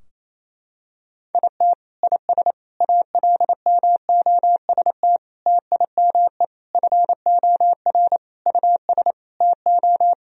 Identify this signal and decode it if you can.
Morse code, heard as a keyed tone, about 28 wpm, 700 Hz. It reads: IT IS ALMOST TIME FOR US TO
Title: Sentences from Top 1000 Words - Review Part 4 - 28wpm